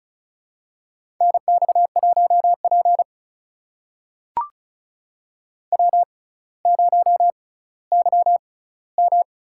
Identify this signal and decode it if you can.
Morse code, heard as a keyed tone, about 36 wpm, spaced as for 18 wpm, 700 Hz. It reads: NX1P W0YM
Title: USA Call Signs - 4 Characters - 6x Character Spacing - 35wpm